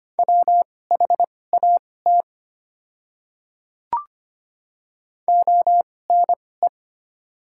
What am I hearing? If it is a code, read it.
Morse code, heard as a keyed tone, about 25 wpm, spaced as for 18 wpm, 700 Hz. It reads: WHAT ONE
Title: 2x Character Spacing - Top 100 Words - 25wpm